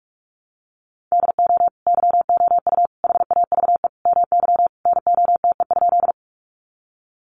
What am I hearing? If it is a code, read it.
Morse code, heard as a keyed tone, about 44 wpm, 700 Hz. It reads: DO YOU HAVE MY NOTE?
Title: Sentences from Top 500 Words 45wpm